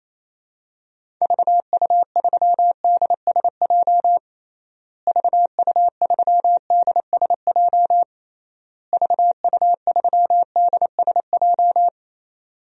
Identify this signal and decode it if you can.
Morse code, heard as a keyed tone, about 28 wpm, 700 Hz. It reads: VU3DSJ VU3DSJ VU3DSJ
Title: DXCC-100 Call Signs - 6 Characters - Repeated 3x - 28wpm